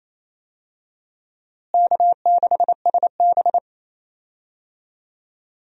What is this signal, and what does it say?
Morse code, heard as a keyed tone, about 27 wpm, 700 Hz. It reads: K6SB
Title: USA Call Signs - 4 Characters - Repeated 3x - 28wpm